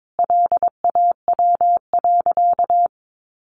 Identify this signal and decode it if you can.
Morse code, heard as a keyed tone, about 22 wpm, 700 Hz. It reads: LAW.
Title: Sentences from Top 1400 Words 22wpm